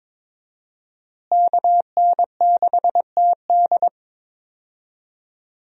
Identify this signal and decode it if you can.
Morse code, heard as a keyed tone, about 22 wpm, 700 Hz. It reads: KN6TD
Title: USA Call Signs - 5 Characters - Repeated 2x - 22wpm